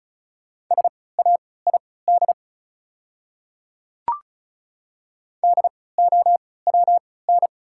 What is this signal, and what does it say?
Morse code, heard as a keyed tone, about 36 wpm, spaced as for 20 wpm, 690 Hz. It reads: SAID DOWN
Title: 3x Character Spacing - Top 100 Words - 35wpm